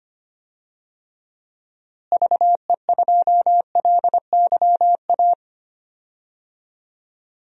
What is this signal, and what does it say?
Morse code, heard as a keyed tone, about 25 wpm, 700 Hz. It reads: VE2LYA
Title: DXCC-100 Call Signs - 6 Characters - Repeated 2x - 25wpm